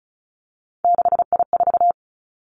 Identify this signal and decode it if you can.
Morse code, heard as a keyed tone, about 36 wpm, 700 Hz. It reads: 6I4
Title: Three Challenging Letter-Number Combinations 35wpm